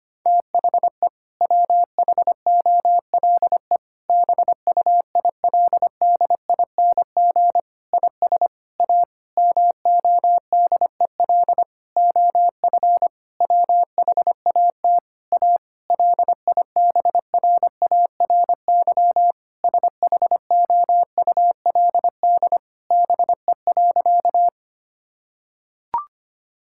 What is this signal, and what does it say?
Morse code, heard as a keyed tone, about 25 wpm, 700 Hz. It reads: THE WHOLE BUILDING IS A MODEL OF WHAT A LIBRARY SHOULD BE.
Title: Sentences from Top 1000 Words - Review Part 5 - 25wpm